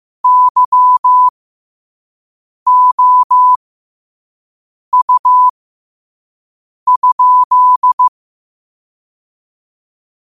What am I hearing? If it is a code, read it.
Morse code, heard as a keyed tone, about 15 wpm, spaced as for 5 wpm, 1000 Hz. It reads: YOU?
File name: qotdfiles / 5wpm